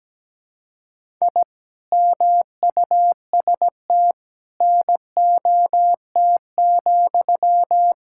Morse code, heard as a keyed tone, about 17 wpm, 700 Hz. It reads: I MUST NOT,
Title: Sentences from Top 1000 Words - Review Part 5 - 17wpm